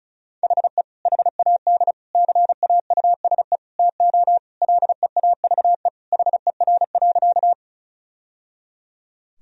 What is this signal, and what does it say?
Morse code, heard as a keyed tone, about 36 wpm, 690 Hz. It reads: HE HAD CAUSE TO LEAVE HER.
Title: Sentences from Top 1000 Words - Review Part 1 - 35wpm